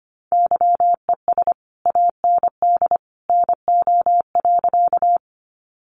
Morse code, heard as a keyed tone, about 25 wpm, 700 Hz. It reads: YES AND NO.
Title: Sentences from Top 500 Words 25wpm